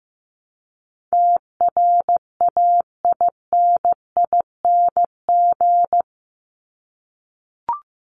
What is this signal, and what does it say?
Morse code, heard as a keyed tone, about 15 wpm, 700 Hz. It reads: TRAINING
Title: Top 501-750 Words 15wpm